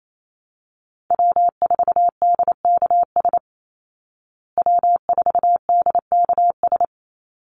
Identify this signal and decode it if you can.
Morse code, heard as a keyed tone, about 28 wpm, 700 Hz. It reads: W4DKS W4DKS
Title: Call Signs Repeated 3x - 28wpm